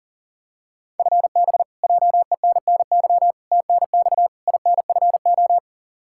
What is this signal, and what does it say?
Morse code, heard as a keyed tone, about 40 wpm, 700 Hz. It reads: FB JENNY TNX INFO
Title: K1USN SST QSO Practice 40wpm